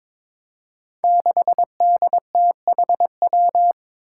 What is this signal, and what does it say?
Morse code, heard as a keyed tone, about 22 wpm, 700 Hz. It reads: 6DTHW
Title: Five Challenging Letter-Number Combinations 22wpm